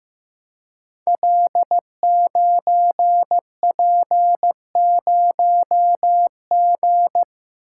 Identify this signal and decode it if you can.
Morse code, heard as a keyed tone, about 15 wpm, 700 Hz. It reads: L9P0G
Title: Code Groups - 5 Letter:Numbers - 15wpm